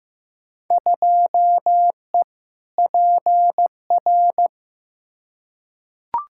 Morse code, heard as a keyed tone, about 15 wpm, 700 Hz. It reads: 2E PR
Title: Field Day Exchanges Encore - 15wpm